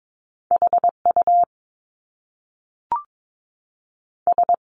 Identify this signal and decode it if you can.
Morse code, heard as a keyed tone, about 22 wpm, 700 Hz. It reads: HU S
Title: Top Two Letter Combinations in Words 22wpm